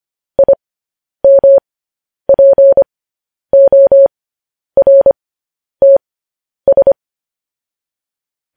Morse code, heard as a keyed tone, about 25 wpm, spaced as for 10 wpm, 550 Hz. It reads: IMPORTS
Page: News Headlines in Morse Code at 10 WPM